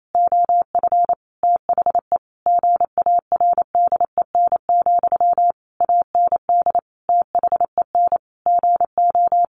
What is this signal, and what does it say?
Morse code, heard as a keyed tone, about 28 wpm, 700 Hz. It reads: OF THE GARDEN, AND THEN GO
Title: Sentences from Top 700 Words 28wpm